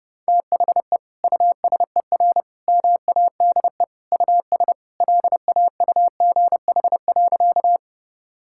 Morse code, heard as a keyed tone, about 30 wpm, 700 Hz. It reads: THE USER MADE US LAUGH.
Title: Sentences from Top 1000 Words - Review Part 5 - 30wpm